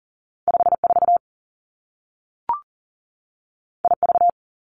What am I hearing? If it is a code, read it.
Morse code, heard as a keyed tone, about 42 wpm, 700 Hz. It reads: 5 4 I V
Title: Two Challenging Letter-Number Combinations 40wpm